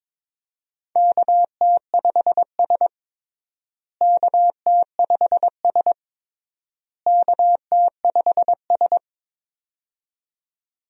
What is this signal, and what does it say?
Morse code, heard as a keyed tone, about 22 wpm, 700 Hz. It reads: KT5S KT5S KT5S
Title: USA Call Signs - 4 Characters - Repeated 3x - 22wpm